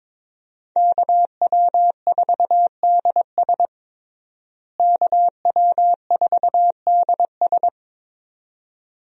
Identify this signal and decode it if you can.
Morse code, heard as a keyed tone, about 22 wpm, 700 Hz. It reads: KW4DS KW4DS
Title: USA Call Signs - 5 Characters - Repeated 2x - 22wpm